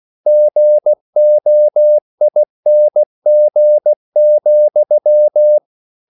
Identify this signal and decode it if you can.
Morse code, heard as a keyed tone, about 16 wpm, 600 Hz. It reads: GOING,